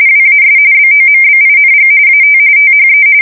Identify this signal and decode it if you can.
RTTY, about 45.45 baud. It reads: WAS BOMBING AT THE
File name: fsk170 / battleshipnj